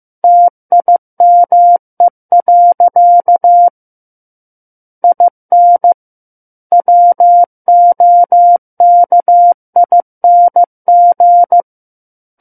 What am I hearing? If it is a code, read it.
Morse code, heard as a keyed tone, about 15 wpm, 700 Hz. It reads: TIME. IN WOKING
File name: Book One / 06-Chapter 6 - 15 WPM